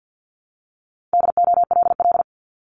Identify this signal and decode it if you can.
Morse code, heard as a keyed tone, about 50 wpm, 700 Hz. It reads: DOLL
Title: Multi-letter Phonemes Repeated 3x 50wpm